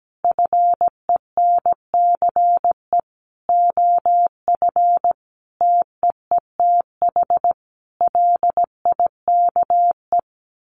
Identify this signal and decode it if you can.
Morse code, heard as a keyed tone, about 17 wpm, 700 Hz. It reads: FENCE OF TEETH LIKE
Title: A Sound Of Thunder 17wpm